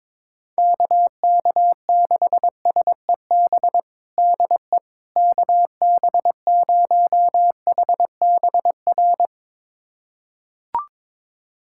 Morse code, heard as a keyed tone, about 22 wpm, 700 Hz. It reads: KK6SEB DE KB0HBR
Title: USA Callsign DE Callsign - 6 Characters - 22wpm